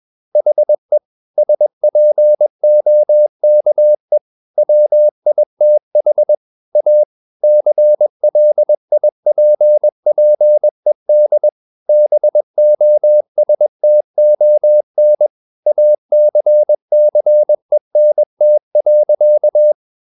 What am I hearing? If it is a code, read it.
Morse code, heard as a keyed tone, about 21 wpm, 600 Hz. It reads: HE SPOKE WITH A CLIPPED BOSTON ACCENT.
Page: CW med Gnister